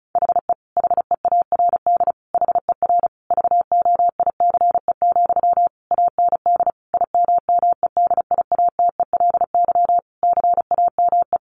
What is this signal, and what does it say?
Morse code, heard as a keyed tone, about 34 wpm, 700 Hz. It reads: HE HEARD HER VOICE, AND IMMEDIATELY CAME
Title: Sentences from Top 1100 Words 35wpm